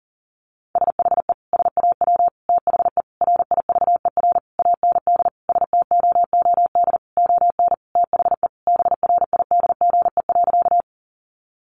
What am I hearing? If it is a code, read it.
Morse code, heard as a keyed tone, about 40 wpm, 700 Hz. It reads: SHE SAW THE RIVER AND STOOD ON THE BRIDGE.
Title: Sentences from Top 1200 Words 40wpm